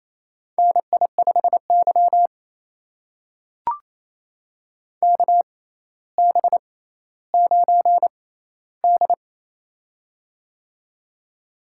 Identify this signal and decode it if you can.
Morse code, heard as a keyed tone, about 28 wpm, spaced as for 14 wpm, 700 Hz. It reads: NI5Y KB9D
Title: USA Call Signs - 4 Characters - 6x Character Spacing - 28wpm